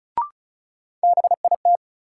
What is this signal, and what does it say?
Morse code, heard as a keyed tone, about 36 wpm, 700 Hz. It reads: BIT
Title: Sets of 3 Words - New Word Drill - 35wpm